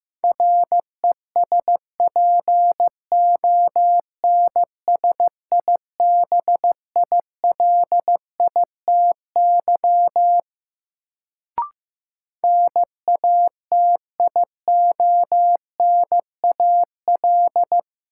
Morse code, heard as a keyed tone, about 15 wpm, 700 Hz. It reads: RESPONSIBILITY NATIONAL
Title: Sentences from Top 1200 Words - New Word Drill Alt - 15wpm